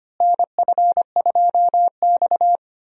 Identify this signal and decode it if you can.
Morse code, heard as a keyed tone, about 25 wpm, 700 Hz. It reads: NF2X
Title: USA Call Signs - 4 Characters - Repeated 2x - 25wpm